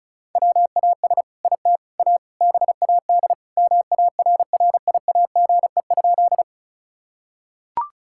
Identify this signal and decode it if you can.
Morse code, heard as a keyed tone, about 36 wpm, 700 Hz. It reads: WAS IT A BAD MARRIAGE?
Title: Sentences from Top 1000 Words - Review Part 5 - 35wpm